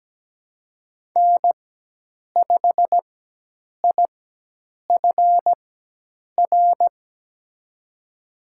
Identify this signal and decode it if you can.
Morse code, heard as a keyed tone, about 17 wpm, spaced as for 8 wpm, 700 Hz. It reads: N5IFR
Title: USA Call Signs - 5 Characters - 4x Character Spacing - 17wpm